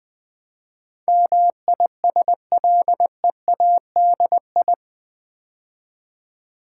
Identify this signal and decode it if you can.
Morse code, heard as a keyed tone, about 20 wpm, 700 Hz. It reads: MISLEADI
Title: Build A Word - 4 Letters At A Time - 10-Letter Words - 20wpm